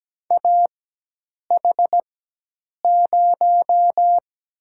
Morse code, heard as a keyed tone, about 17 wpm, 700 Hz. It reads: A H 0
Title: USA Call Signs - 4 Characters - 4x Character Spacing - 17wpm